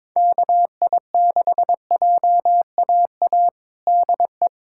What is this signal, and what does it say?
Morse code, heard as a keyed tone, about 22 wpm, 700 Hz. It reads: KI6JAA DE
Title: USA Callsign DE Callsign 22wpm